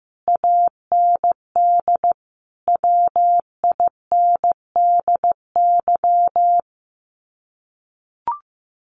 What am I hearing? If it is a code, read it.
Morse code, heard as a keyed tone, about 15 wpm, 700 Hz. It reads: AND WINDY
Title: Common QSO Phrases Encore 15wpm